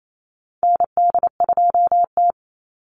Morse code, heard as a keyed tone, about 28 wpm, 700 Hz. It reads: ND2T
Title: Call Signs Repeated 2x - 28wpm